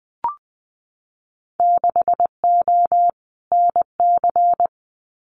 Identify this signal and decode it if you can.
Morse code, heard as a keyed tone, about 20 wpm, 700 Hz. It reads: 6O NC
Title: Winter Field Day Exchanges 20wpm